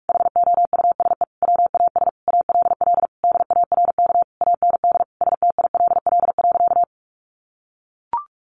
Morse code, heard as a keyed tone, about 44 wpm, 700 Hz. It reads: HOUSE WAS ALL DARK AND STILL.
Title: Sentences from Top 500 Words 45wpm